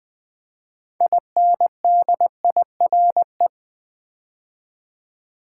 Morse code, heard as a keyed tone, about 20 wpm, 700 Hz. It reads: INDIRE
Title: Build A Word - 2 Letters At A Time - 8-Letter Words - 20wpm